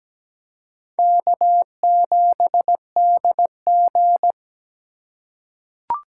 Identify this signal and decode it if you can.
Morse code, heard as a keyed tone, about 17 wpm, 700 Hz. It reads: K7DG
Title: USA Call Signs - 4 Characters - 4x Character Spacing - 17wpm